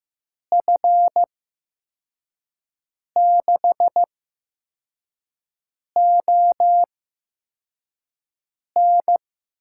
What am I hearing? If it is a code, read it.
Morse code, heard as a keyed tone, about 15 wpm, spaced as for 4 wpm, 700 Hz. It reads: F6ON
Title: USA Call Signs - 5 Characters - 8x Character Spacing - 15wpm